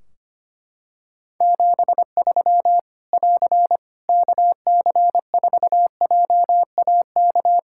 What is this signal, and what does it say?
Morse code, heard as a keyed tone, about 25 wpm, 700 Hz. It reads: 73 <AR> KC4JAK
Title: Practice QSO - Exchange 4 - Ending - 25wpm